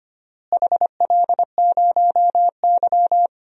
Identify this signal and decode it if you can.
Morse code, heard as a keyed tone, about 25 wpm, 700 Hz. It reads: HL0Y
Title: DXCC-100 Call Signs - 4 Characters - Repeated 2x - 25wpm